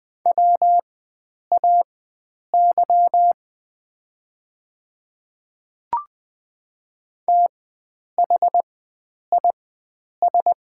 Morse code, heard as a keyed tone, about 20 wpm, spaced as for 9 wpm, 700 Hz. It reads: WAY THIS
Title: 4x Character Spacing - Top 100 Words - 20wpm